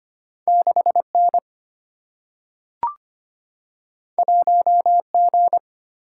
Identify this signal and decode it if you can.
Morse code, heard as a keyed tone, about 25 wpm, 700 Hz. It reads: 6N 1G
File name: Two Letter-Number Combinations - 25wpm